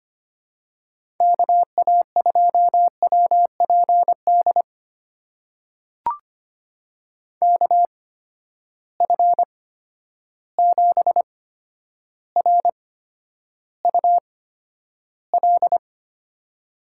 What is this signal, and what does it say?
Morse code, heard as a keyed tone, about 25 wpm, spaced as for 11 wpm, 700 Hz. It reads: KA2WPD KF7RUL
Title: USA Call Signs - 6 Characters - 8x Character Spacing - 25wpm